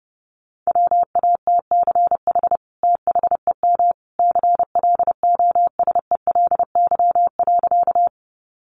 Morse code, heard as a keyed tone, about 30 wpm, 700 Hz. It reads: WATCH THEM CLOSELY.
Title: Sentences from Top 1400 Words 30wpm